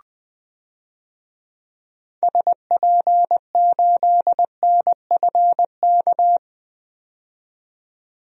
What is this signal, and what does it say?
Morse code, heard as a keyed tone, about 20 wpm, 700 Hz. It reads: SP8NFK
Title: DXCC-100 Call Signs - 6 Characters - 20wpm